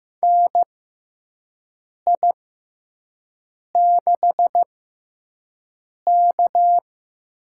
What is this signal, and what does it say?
Morse code, heard as a keyed tone, about 15 wpm, spaced as for 5 wpm, 700 Hz. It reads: NI6K
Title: USA Call Signs - 4 Characters - 6x Character Spacing - 15wpm